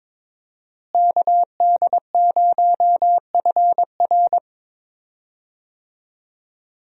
Morse code, heard as a keyed tone, about 22 wpm, 700 Hz. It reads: KD0FR
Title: USA Call Signs - 5 Characters - Repeated 3x - 22wpm